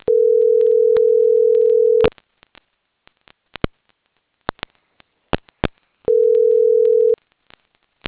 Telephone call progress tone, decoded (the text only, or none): ringback tone